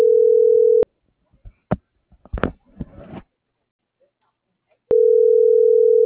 Ringback tone (North America), 440 + 480 Hz.